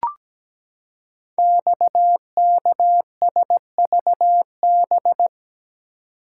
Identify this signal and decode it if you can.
Morse code, heard as a keyed tone, about 17 wpm, 700 Hz. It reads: XKSVB
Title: Code Groups - 5 Letters - 17wpm